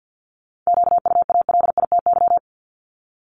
Morse code, heard as a keyed tone, about 50 wpm, 700 Hz. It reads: QUALITY
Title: Multi-letter Phonemes Repeated 3x 50wpm